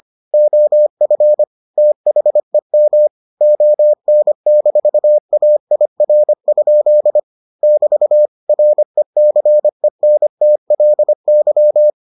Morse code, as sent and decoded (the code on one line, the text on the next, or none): --- ..-. / - .... -- / --- -. -....- .- .. .-. ..--.. / -...- / .-. -.-. -. - .-.. -.--
OF THEM ON-AIR? <BT> RECENTLY